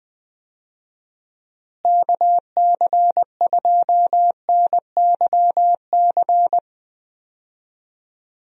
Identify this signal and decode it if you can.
Morse code, heard as a keyed tone, about 21 wpm, 700 Hz. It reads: KC2NYC
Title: USA Call Signs - 6 Characters - Repeated 3x - 20wpm